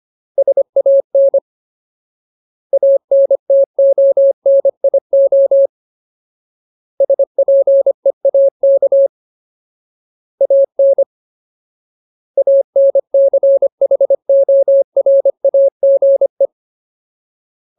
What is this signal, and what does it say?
Morse code, heard as a keyed tone, about 25 wpm, 550 Hz. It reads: SAN ANTONIO SPEAK AN ANCHORAGE
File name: Ultimate Word List / 25WPM Ultimate